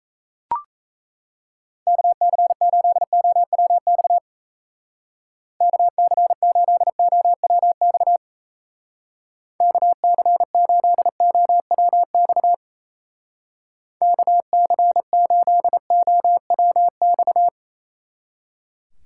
Morse code, heard as a keyed tone, about 35 wpm, 700 Hz. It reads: KC8OWX KC8OWX KC8OWX KC8OWX
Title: Speed Racing - USA 2x3 Call Signs 28wpm